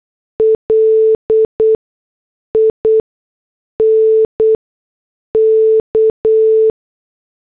Morse code, heard as a keyed tone, about 8 wpm, 440 Hz. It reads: L I N K